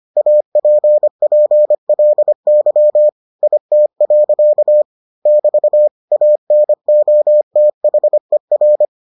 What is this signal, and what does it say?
Morse code, heard as a keyed tone, about 25 wpm, 600 Hz. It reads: APPLY IT. <BT> ANOTHER